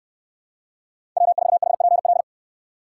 Morse code, heard as a keyed tone, about 69 wpm, 700 Hz. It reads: W4HPL